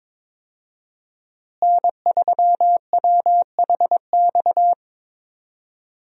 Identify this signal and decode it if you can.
Morse code, heard as a keyed tone, about 22 wpm, 700 Hz. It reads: N3WHX
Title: USA Call Signs - 5 Characters - Repeated 3x - 22wpm